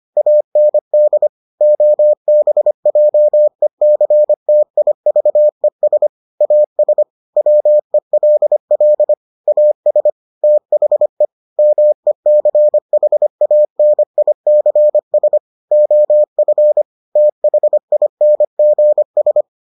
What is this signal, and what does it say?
Morse code, heard as a keyed tone, about 25 wpm, 600 Hz. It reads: AND OBJECTIVES AS WELL AS THE MECHANICS OF THINGS